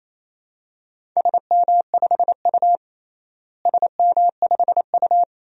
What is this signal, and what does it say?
Morse code, heard as a keyed tone, about 28 wpm, 700 Hz. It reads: SM5U SM5U
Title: DXCC-100 Call Signs - 4 Characters - Repeated 3x - 28wpm